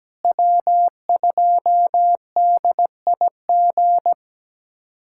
Morse code, heard as a keyed tone, about 17 wpm, 700 Hz. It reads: W2DIG
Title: USA Callsign DE Callsign - 5 Characters - 17wpm